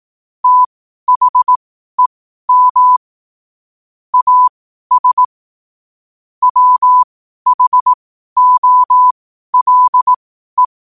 Morse code, heard as a keyed tone, about 18 wpm, spaced as for 12 wpm, 1000 Hz. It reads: THEM AS WHOLE
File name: qotdfiles / 12wpm